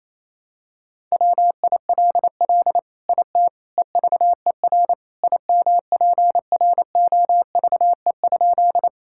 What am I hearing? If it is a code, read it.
Morse code, heard as a keyed tone, about 28 wpm, 700 Hz. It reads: WILL IT EVER IMPROVE?
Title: Sentences from Top 1000 Words - Review Part 5 - 28wpm